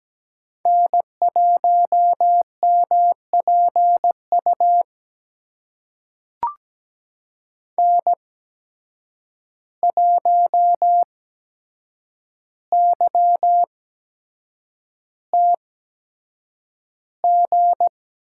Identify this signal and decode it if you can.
Morse code, heard as a keyed tone, about 17 wpm, spaced as for 7 wpm, 700 Hz. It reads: N1MPU N1YTG
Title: USA Call Signs - 5 Characters - 8x Character Spacing - 17wpm